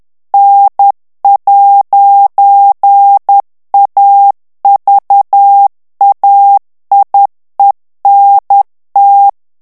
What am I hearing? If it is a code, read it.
Morse code, heard as a keyed tone, about 10 wpm, 800 Hz. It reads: N'AVAIENT
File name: FRANCAIS TEXTE CLAIR 2 VITESSE 10 MOTS MINUTES